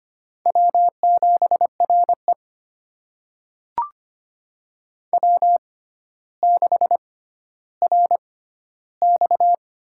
Morse code, heard as a keyed tone, about 25 wpm, spaced as for 13 wpm, 700 Hz. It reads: W7RE W6RX
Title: USA Call Signs - 4 Characters - 6x Character Spacing - 25wpm